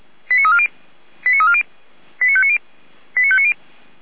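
Five-tone selective call (ZVEI), heard as quick runs of five tones, tones of about 70 ms. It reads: 89340 89340 89590 89590
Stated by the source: ZVEI five-tone selective call 89305 received on an SDR, logged by Feuerwehr Orlamünde (feuerwehr-orlamuende.org).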